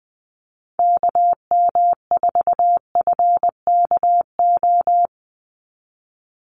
Morse code, heard as a keyed tone, about 20 wpm, 700 Hz. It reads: KM4FKO
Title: USA Call Signs 20wpm